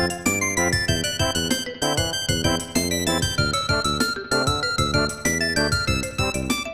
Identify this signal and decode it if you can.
There is no code.